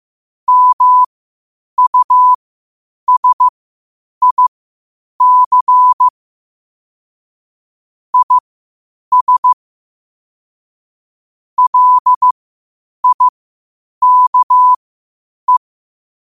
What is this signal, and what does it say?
Morse code, heard as a keyed tone, about 15 wpm, spaced as for 8 wpm, 1000 Hz. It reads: MUSIC IS LIKE
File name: qotdfiles / 8wpm